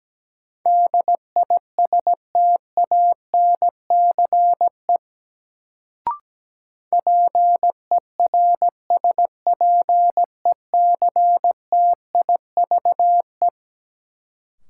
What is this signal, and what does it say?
Morse code, heard as a keyed tone, about 17 wpm, 700 Hz. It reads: DISTANCE PERSPECTIVE
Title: Sentences from Top 1200 Words - New Word Drill - 17wpm